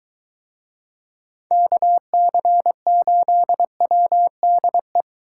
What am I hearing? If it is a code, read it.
Morse code, heard as a keyed tone, about 21 wpm, 700 Hz. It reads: KC8WDE